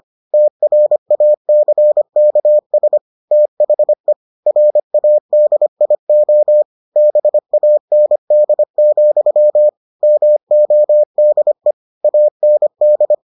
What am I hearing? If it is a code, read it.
Morse code, heard as a keyed tone, about 25 wpm, 600 Hz. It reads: TRACKS THE RADIO BAND, MODE AND